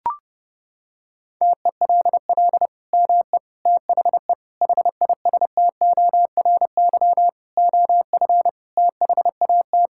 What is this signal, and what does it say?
Morse code, heard as a keyed tone, about 32 wpm, 700 Hz. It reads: TELL ME THE HISTORY OF THAT
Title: Sentences from Top 1000 Words - Review Part 5 - 30wpm